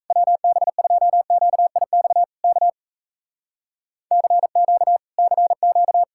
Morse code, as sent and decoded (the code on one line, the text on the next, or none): .-- -... ..--- --.- .. -..- / -.- / -.-. --.- / -.-. --.-
WB2QIX K CQ CQ